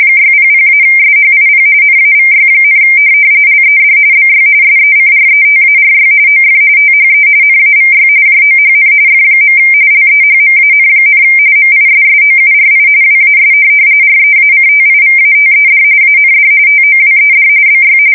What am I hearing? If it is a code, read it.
RTTY, about 45.45 baud. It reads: 600-SHIP NAVY, LED BY SECRETARY OF THE NAVY JOHN F. LEHMAN, MISSOURI WAS REACTIVATED AND TOWED BY
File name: fsk170 / USS-Missouri